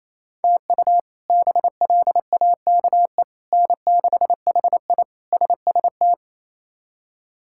Morse code, heard as a keyed tone, about 28 wpm, 700 Hz. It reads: TU BLAKE N6HI SST
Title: K1USN SST QSO Practice - 28wpm